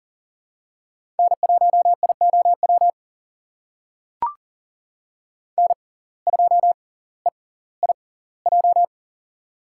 Morse code, heard as a keyed tone, about 40 wpm, spaced as for 20 wpm, 700 Hz. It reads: N1IOW N2EIJ
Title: USA Call Signs - 5 Characters - 6x Character Spacing - 40wpm